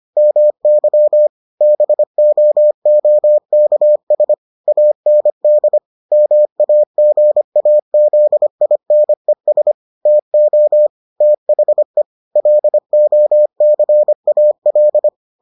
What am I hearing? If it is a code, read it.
Morse code, heard as a keyed tone, about 25 wpm, 600 Hz. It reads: MY BOOKS AND MAGAZINES TO THE LOCAL